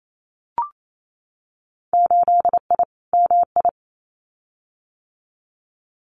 Morse code, heard as a keyed tone, about 28 wpm, 700 Hz. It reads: 8I MI
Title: Winter Field Day Exchanges Encore 28wpm